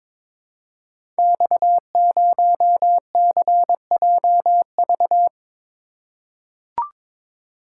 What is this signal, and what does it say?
Morse code, heard as a keyed tone, about 22 wpm, 700 Hz. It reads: X0CJV
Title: Code Groups - 5 Letter:Numbers - 22wpm